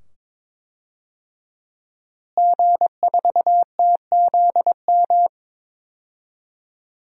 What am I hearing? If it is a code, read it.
Morse code, heard as a keyed tone, about 22 wpm, 700 Hz. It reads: G4TZM
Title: DXCC-100 Call Signs - 5 Characters - Repeated 2x - 22wpm